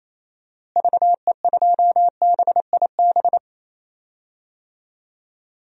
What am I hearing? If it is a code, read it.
Morse code, heard as a keyed tone, about 27 wpm, 700 Hz. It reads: VE2BIB
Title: DXCC-100 Call Signs - 6 Characters - Repeated 2x - 28wpm